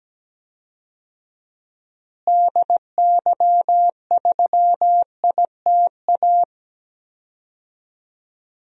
Morse code, heard as a keyed tone, about 17 wpm, 700 Hz. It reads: DY3ITA
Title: DXCC-100 Call Signs - 6 Characters - Repeated 3x - 17wpm